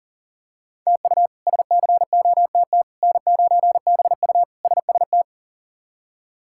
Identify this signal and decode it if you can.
Morse code, heard as a keyed tone, about 40 wpm, 700 Hz. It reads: TU SCOTT N9BU SST
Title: K1USN SST QSO Practice - 40wpm